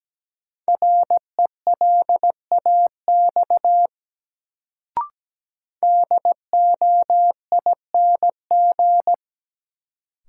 Morse code, heard as a keyed tone, about 17 wpm, 700 Hz. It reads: RELAX DOING
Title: Sentences from Top 1400 Words - New Word Drill Alt - 17wpm